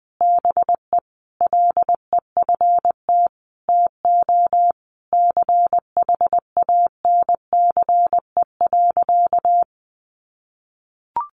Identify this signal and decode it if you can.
Morse code, heard as a keyed tone, about 20 wpm, 700 Hz. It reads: BE LEFT TO CHANCE.
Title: Sentences from Top 1100 Words 20wpm